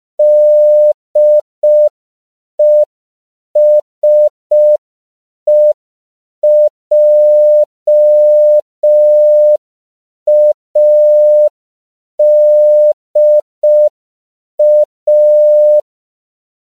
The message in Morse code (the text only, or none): DESEJADA